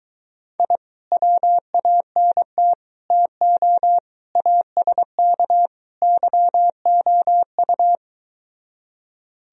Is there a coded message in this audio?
Yes: Morse code from 0.6 to 8.0 s, heard as a keyed tone, about 23 wpm, 700 Hz. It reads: I WANT TO ASK YOU